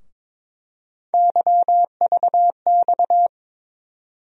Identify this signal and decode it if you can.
Morse code, heard as a keyed tone, about 22 wpm, 700 Hz. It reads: YVX
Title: USA Call Sign Postfixes - 3 Letters 22wpm